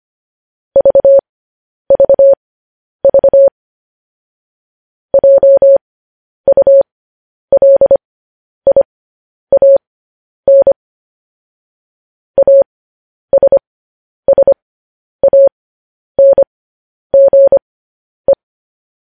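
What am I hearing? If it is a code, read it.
Morse code, heard as a keyed tone, about 25 wpm, spaced as for 10 wpm, 550 Hz. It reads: VVV JULIAN ASSANGE
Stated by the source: Morse code news headlines at 10 wpm.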